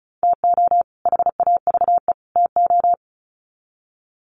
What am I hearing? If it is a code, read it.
Morse code, heard as a keyed tone, about 34 wpm, 700 Hz. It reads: TO HAVE TO
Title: Sets of 3 Words 35wpm